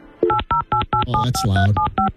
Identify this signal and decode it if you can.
DTMF, heard as telephone touch tones, tones of about 120 ms. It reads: 808875375